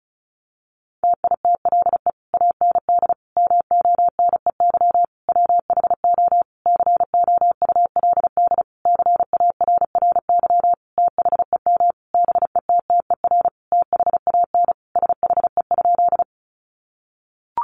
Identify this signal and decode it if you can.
Morse code, heard as a keyed tone, about 34 wpm, 700 Hz. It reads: TITLE AND MONEY WHO COULD CARRY THEM BETTER THAN SHE?
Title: Sentences from Top 1200 Words 35wpm